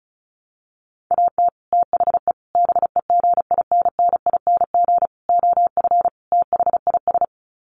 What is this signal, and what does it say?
Morse code, heard as a keyed tone, about 34 wpm, 700 Hz. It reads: AT THE BEGINNING OF THIS
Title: Sets of 5 Words Encore 35wpm